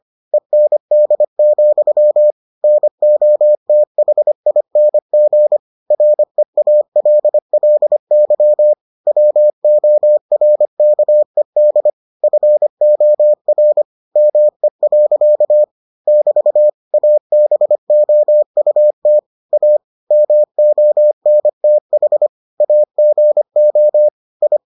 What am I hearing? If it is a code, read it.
Morse code, heard as a keyed tone, about 25 wpm, 600 Hz. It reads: END, NOTHING REALLY WORKED FOR ME. <BT> ABOUT A MONTH AGO I